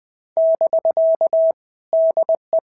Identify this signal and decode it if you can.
Morse code, heard as a keyed tone, about 20 wpm, 650 Hz. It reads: <BK> DE
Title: POTA QSO Practice - 20wpm